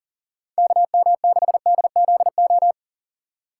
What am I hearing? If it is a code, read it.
Morse code, heard as a keyed tone, about 40 wpm, 700 Hz. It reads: KM6DZO
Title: USA Call Signs - 3x Character Spacing - 40wpm